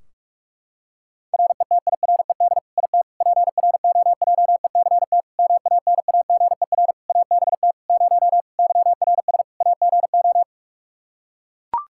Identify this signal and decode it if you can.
Morse code, heard as a keyed tone, about 44 wpm, 700 Hz. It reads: RETIRED IT PROJECT MANAGER ABT 0 YRS AGO
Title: Practice QSO - Exchange 3 - More Info - 45wpm